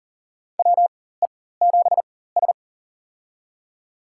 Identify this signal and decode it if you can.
Morse code, heard as a keyed tone, about 40 wpm, spaced as for 19 wpm, 700 Hz. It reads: WE7S
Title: USA Call Signs - 4x Character Spacing - 40wpm